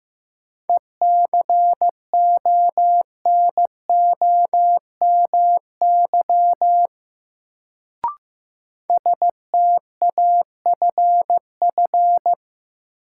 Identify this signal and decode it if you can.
Morse code, heard as a keyed tone, about 15 wpm, 700 Hz. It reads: ECONOMY STAFF
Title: Sentences from Top 1200 Words - New Word Drill Alt - 15wpm